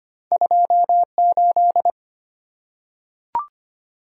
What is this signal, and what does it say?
Morse code, heard as a keyed tone, about 25 wpm, 690 Hz. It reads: 28
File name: Two Letter-Number Combinations - 25wpm